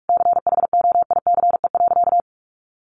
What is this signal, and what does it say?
Morse code, heard as a keyed tone, about 44 wpm, 700 Hz. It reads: CHOICE.
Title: Sentences from Top 900 Words 45wpm